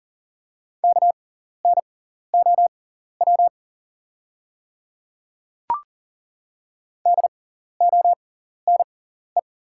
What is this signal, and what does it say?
Morse code, heard as a keyed tone, about 40 wpm, spaced as for 14 wpm, 690 Hz. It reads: KNOW DONE
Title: 6x Character Spacing - Top 500 Words - 40wpm